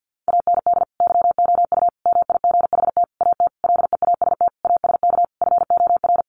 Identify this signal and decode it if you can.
Morse code, heard as a keyed tone, about 50 wpm, 700 Hz. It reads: AND YOU MIGHT AT LEAST ASK FOR
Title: Sentences from Top 1500 Words 50wpm